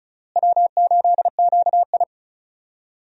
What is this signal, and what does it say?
Morse code, heard as a keyed tone, about 34 wpm, 700 Hz. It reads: W8QI
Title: USA Call Signs - 4 Characters - Repeated 3x - 35wpm